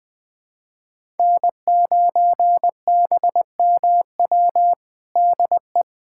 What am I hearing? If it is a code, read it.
Morse code, heard as a keyed tone, about 21 wpm, 700 Hz. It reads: N9BMW DE
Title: USA Callsign DE Callsign 20wpm